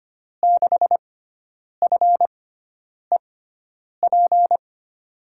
Morse code, heard as a keyed tone, about 25 wpm, spaced as for 9 wpm, 700 Hz. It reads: 6FEP
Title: USA Call Signs - 6 Characters - 6x Character Spacing - 25wpm